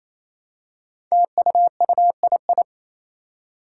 Morse code, heard as a keyed tone, about 27 wpm, 700 Hz. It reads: TUUII
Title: Five Challenging Letter-Number Combinations 28wpm